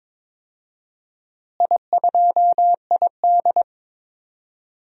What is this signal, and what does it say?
Morse code, heard as a keyed tone, about 22 wpm, 700 Hz. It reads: I2ID
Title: DXCC-100 Call Signs - 4 Characters - Repeated 2x - 22wpm